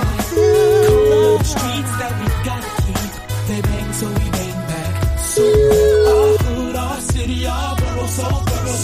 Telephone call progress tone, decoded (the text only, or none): ringback tone